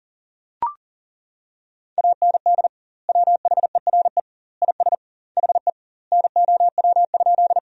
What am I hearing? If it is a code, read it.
Morse code, heard as a keyed tone, about 40 wpm, 700 Hz. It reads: AND WHERE IS HE NOW?
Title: 2x Word-Spacing - Sentences from Top 200 Words - 40wpm